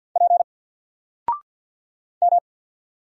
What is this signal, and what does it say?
Morse code, heard as a keyed tone, about 50 wpm, 700 Hz. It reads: P M
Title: Single Letter-Number Alt 50wpm